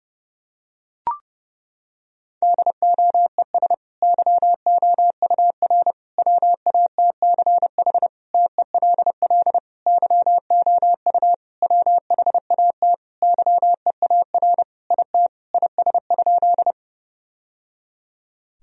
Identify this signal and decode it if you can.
Morse code, heard as a keyed tone, about 30 wpm, 700 Hz. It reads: DOES YOUR WATCH TELL YOU WHAT YEAR IT IS?
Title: Sentences from Top 1000 Words - Review Part 1 - 30wpm